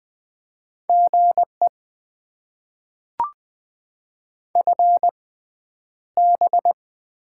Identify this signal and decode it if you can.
Morse code, heard as a keyed tone, about 20 wpm, spaced as for 10 wpm, 700 Hz. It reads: GE FB
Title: Top 20 QSO Elements - 6x Character Spacing - 20wpm